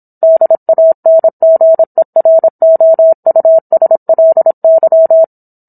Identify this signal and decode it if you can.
Morse code, heard as a keyed tone, about 27 wpm, 650 Hz. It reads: DANGEROUSLY